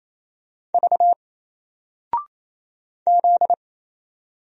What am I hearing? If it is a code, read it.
Morse code, heard as a keyed tone, about 28 wpm, 700 Hz. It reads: V Z
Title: Single Character (Letter:Number:Punctuation:Prosign) 28wpm